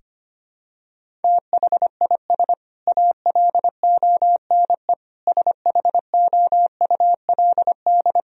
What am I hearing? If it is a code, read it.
Morse code, heard as a keyed tone, about 25 wpm, 700 Hz. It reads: THIS ALONE SHOULD
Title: Sentences from Top 1000 Words - Review Part 4 - 25wpm